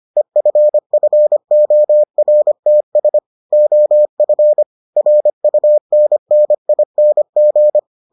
Morse code, heard as a keyed tone, about 25 wpm, 600 Hz. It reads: EFFORTS OF RUNNING